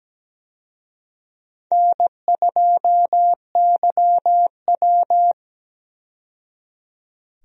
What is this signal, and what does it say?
Morse code, heard as a keyed tone, about 17 wpm, 700 Hz. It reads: N2YW
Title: USA Call Signs - 4 Characters - 17wpm